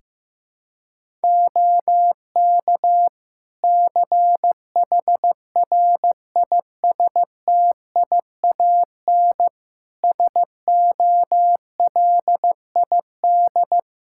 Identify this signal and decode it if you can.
Morse code, heard as a keyed tone, about 15 wpm, 700 Hz. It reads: OK CHRISTIAN SOLID
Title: Practice QSO - Exchange 3 - More Info - 15wpm